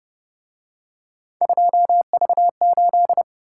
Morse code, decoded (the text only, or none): 2V8